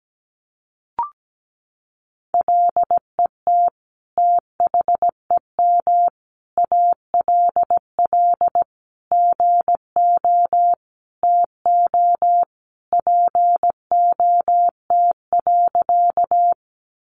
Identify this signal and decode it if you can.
Morse code, heard as a keyed tone, about 17 wpm, 700 Hz. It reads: LET THEM ALL GO TO POT.
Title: Sentences from Top 1000 Words 17wpm